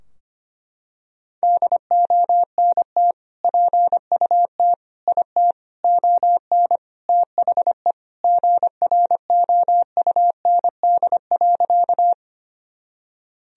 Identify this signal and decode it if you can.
Morse code, heard as a keyed tone, about 25 wpm, 700 Hz. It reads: DONT PUT IT ON THE GROUND.
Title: Sentences from Top 1000 Words - Review Part 2 - 25wpm